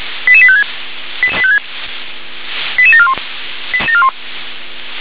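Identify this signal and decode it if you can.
Five-tone selective call (ZVEI), heard as quick runs of five tones, tones of about 70 ms. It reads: 88856 88741
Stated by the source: ZVEI five-tone call 88796 received on an SDR.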